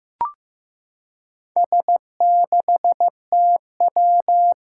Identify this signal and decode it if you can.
Morse code, heard as a keyed tone, about 15 wpm, 700 Hz. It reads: S6TW
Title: Four Challenging Letter-Number Combinations 15wpm